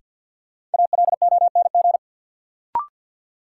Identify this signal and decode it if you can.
Morse code, heard as a keyed tone, about 53 wpm, 700 Hz. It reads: ALONG
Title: Sentences from Top 700 Words - New Word Drill - 50wpm